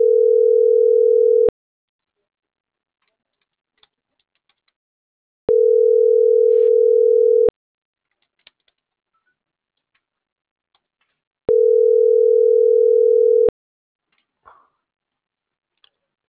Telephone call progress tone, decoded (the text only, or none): ringback tone